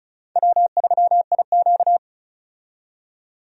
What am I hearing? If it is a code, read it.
Morse code, heard as a keyed tone, about 36 wpm, 700 Hz. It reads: W3IQ
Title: USA Call Signs - 4 Characters - 35wpm